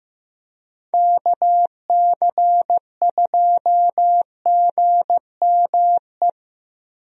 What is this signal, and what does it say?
Morse code, heard as a keyed tone, about 15 wpm, 700 Hz. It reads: KC2GME
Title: USA Call Signs - 6 Characters - Repeated 2x - 15wpm